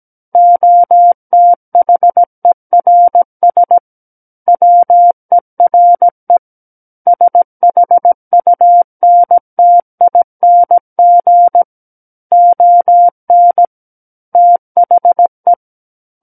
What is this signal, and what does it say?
Morse code, heard as a keyed tone, about 17 wpm, 700 Hz. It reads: OTHERS WERE SHUNTING ON THE
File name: Book One / 08-Chapter 8 - 17 WPM